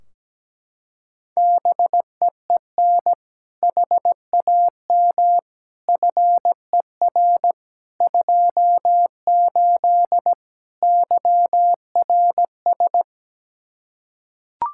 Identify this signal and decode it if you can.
Morse code, heard as a keyed tone, about 17 wpm, 700 Hz. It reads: BEEN HAM FER 28 YRS
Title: Practice QSO - Exchange 3 - More Info - 17wpm